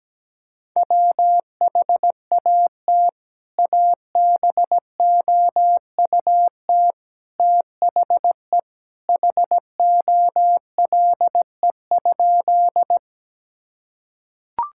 Morse code, heard as a keyed tone, about 17 wpm, 700 Hz. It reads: WHAT ABOUT THE HOLE?
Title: Sentences from Top 1600 Words 17wpm